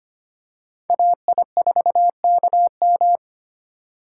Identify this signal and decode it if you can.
Morse code, heard as a keyed tone, about 25 wpm, 700 Hz. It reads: AI4KM
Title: USA Call Signs - 5 Characters - 3x Character Spacing - 25wpm